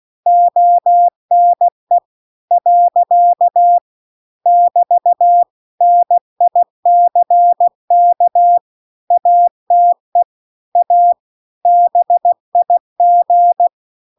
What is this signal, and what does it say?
Morse code, heard as a keyed tone, about 16 wpm, 700 Hz. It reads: ONE . <BT> NICK ATE A BIG